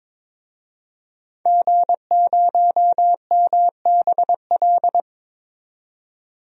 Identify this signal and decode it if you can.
Morse code, heard as a keyed tone, about 22 wpm, 700 Hz. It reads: G0MBL
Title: DXCC-100 Call Signs - 5 Characters - 22wpm